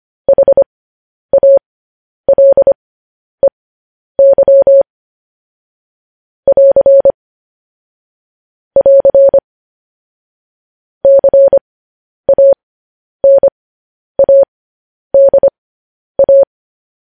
Morse code, heard as a keyed tone, about 25 wpm, spaced as for 10 wpm, 550 Hz. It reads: HALEY <AR> <AR> CANADA